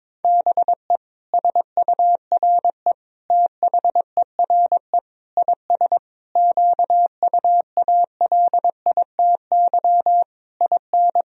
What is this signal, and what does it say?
Morse code, heard as a keyed tone, about 22 wpm, 700 Hz. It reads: BE SURE THERE IS QUALITY IN
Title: Sentences from Top 1000 Words - Review Part 4 - 22wpm